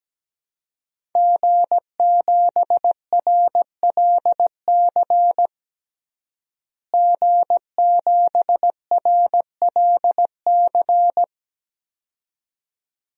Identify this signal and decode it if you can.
Morse code, heard as a keyed tone, about 17 wpm, 700 Hz. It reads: G7RLC G7RLC
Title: DXCC-100 Call Signs - 5 Characters - Repeated 2x - 17wpm